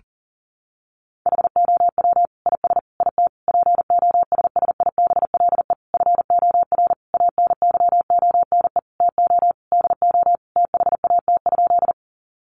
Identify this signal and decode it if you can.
Morse code, heard as a keyed tone, about 40 wpm, 700 Hz. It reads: HOW IS IT POSSIBLE FOR ANYONE TO DO THAT?
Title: Sentences from Top 900 Words 40wpm